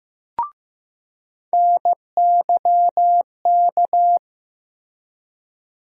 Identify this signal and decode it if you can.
Morse code, heard as a keyed tone, about 15 wpm, 700 Hz. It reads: NYK
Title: USA Call Sign Postfixes - 3 Letters 15wpm